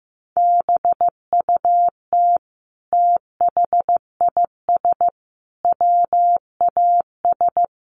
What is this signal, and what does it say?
Morse code, heard as a keyed tone, about 15 wpm, 700 Hz. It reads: BUT THIS WAS
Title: Sentences from Top 1500 Words 15wpm